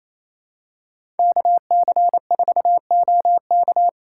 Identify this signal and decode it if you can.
Morse code, heard as a keyed tone, about 28 wpm, 700 Hz. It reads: KC4OK
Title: USA Call Signs - 5 Characters - 2x Character Spacing - 28wpm